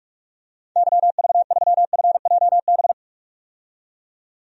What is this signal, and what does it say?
Morse code, heard as a keyed tone, about 46 wpm, 700 Hz. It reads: YV3FJB
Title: DXCC-100 Call Signs - 6 Characters - 45wpm